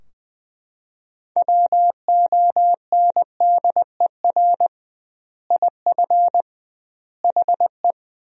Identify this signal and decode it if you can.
Morse code, heard as a keyed tone, about 20 wpm, 700 Hz. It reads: WONDER IF HE
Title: 2x Word-Spacing - Sentences from Top 500 Words - 20wpm